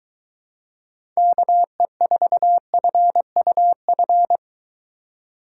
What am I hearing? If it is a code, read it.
Morse code, heard as a keyed tone, about 23 wpm, 700 Hz. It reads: KE4FUF